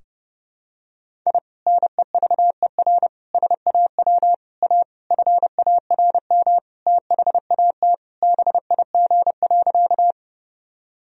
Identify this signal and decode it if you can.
Morse code, heard as a keyed tone, about 30 wpm, 700 Hz. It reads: I NEVER SAW A FARM THAT BIG.
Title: Sentences from Top 1000 Words - Review Part 2 - 30wpm